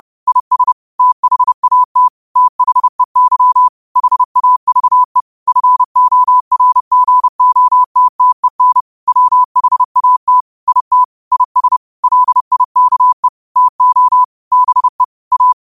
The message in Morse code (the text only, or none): IS THAT THEY HAVE FORGOTTEN WHAT IT IS LIKE TO BE A